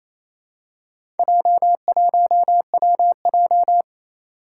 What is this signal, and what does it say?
Morse code, heard as a keyed tone, about 27 wpm, 700 Hz. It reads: J1WJ
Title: Four Challenging Letter-Number Combinations 28wpm